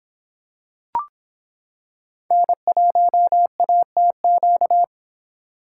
Morse code, heard as a keyed tone, about 26 wpm, 700 Hz. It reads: N1ATQ